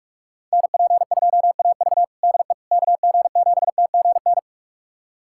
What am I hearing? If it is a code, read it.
Morse code, heard as a keyed tone, about 46 wpm, 700 Hz. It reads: NP2AV DE KG7TGN
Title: USA Callsign DE Callsign 45wpm